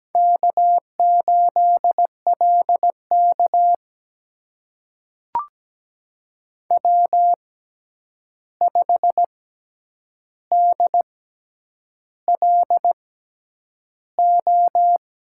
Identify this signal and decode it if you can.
Morse code, heard as a keyed tone, about 17 wpm, spaced as for 8 wpm, 700 Hz. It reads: K8LK W5DLO
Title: USA Call Signs - 6x Character Spacing - 17wpm